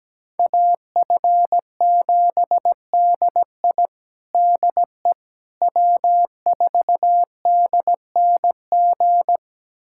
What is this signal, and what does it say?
Morse code, heard as a keyed tone, about 17 wpm, 700 Hz. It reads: AF7DI DE W4DNG
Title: USA Callsign DE Callsign - 5 Characters - 17wpm